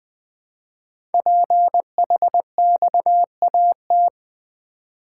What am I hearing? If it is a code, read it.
Morse code, heard as a keyed tone, about 20 wpm, 700 Hz. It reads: PHXAT
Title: Code Groups - 5 Letter:Numbers - 20wpm